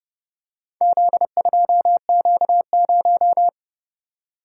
Morse code, heard as a keyed tone, about 30 wpm, 700 Hz. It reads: Z2Q0